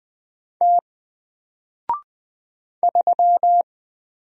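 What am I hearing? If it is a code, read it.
Morse code, heard as a keyed tone, about 20 wpm, 700 Hz. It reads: T 3
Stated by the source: Single letter and number Morse drill at 20 wpm, from N4LCD.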